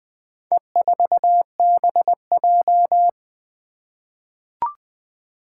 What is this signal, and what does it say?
Morse code, heard as a keyed tone, about 20 wpm, 700 Hz. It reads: E4BJ
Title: Four Challenging Letter-Number Combinations 20wpm